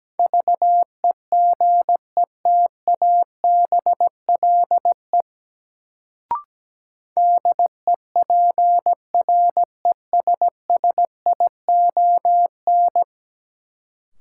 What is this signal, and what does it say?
Morse code, heard as a keyed tone, about 17 wpm, 700 Hz. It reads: VEGETABLE DEPRESSION
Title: Sentences from Top 1300 Words - New Word Drill - 17wpm